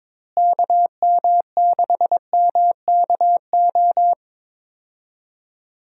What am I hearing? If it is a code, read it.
Morse code, heard as a keyed tone, about 22 wpm, 700 Hz. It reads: KM6MKO
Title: USA Call Signs - 6 Characters - Repeated 3x - 22wpm